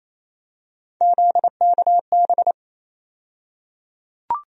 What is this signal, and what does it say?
Morse code, heard as a keyed tone, about 28 wpm, 700 Hz. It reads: ZKB
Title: USA Call Sign Postfixes - 3 Letters 28wpm